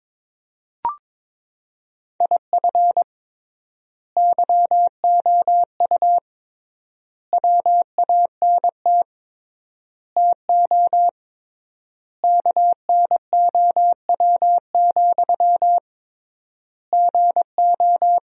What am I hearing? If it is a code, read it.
Morse code, heard as a keyed tone, about 22 wpm, 700 Hz. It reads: IF YOU WANT TO KNOW, GO
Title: 3x Word-Spacing - Sentences from Top 200 Words - 22wpm